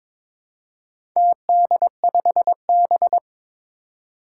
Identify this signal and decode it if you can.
Morse code, heard as a keyed tone, about 21 wpm, 700 Hz. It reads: TD5B
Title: Four Challenging Letter-Number Combinations 22wpm